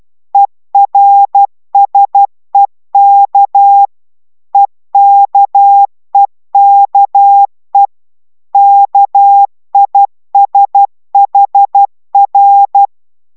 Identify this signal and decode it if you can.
Morse code, heard as a keyed tone, about 12 wpm, 800 Hz. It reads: ERSEK EKEKE KISHR